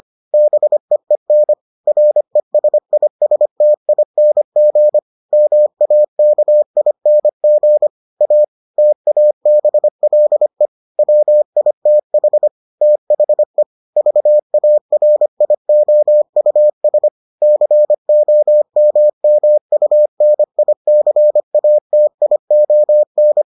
Morse code, heard as a keyed tone, about 25 wpm, 600 Hz. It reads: BEEN RESISTING MAKING A TABLE WITH THE VARIOUS COMMUNICATION